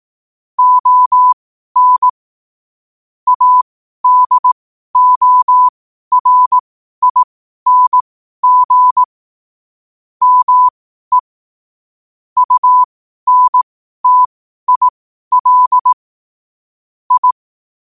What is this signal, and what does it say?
Morse code, heard as a keyed tone, about 18 wpm, spaced as for 12 wpm, 1000 Hz. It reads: ON ADORING ME UNTIL I
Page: Quotes for Fri, 15 Aug 2025 in Morse Code at 12 words per minute.